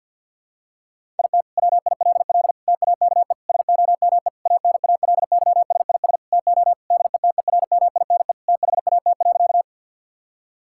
Morse code, heard as a keyed tone, about 48 wpm, 690 Hz. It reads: IT WILL TAKE SOME ANALYSIS TO DETERMINE THAT.
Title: Sentences from Top 1000 Words - Review Part 5 - 50wpm